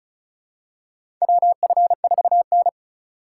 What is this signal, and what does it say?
Morse code, heard as a keyed tone, about 34 wpm, 700 Hz. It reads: WF4N
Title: USA Call Signs - 4 Characters - 2x Character Spacing - 35wpm